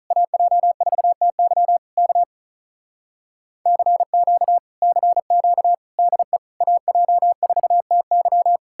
Morse code, heard as a keyed tone, about 48 wpm, 700 Hz. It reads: AJ4TY K CQ CQ DE AJ4TY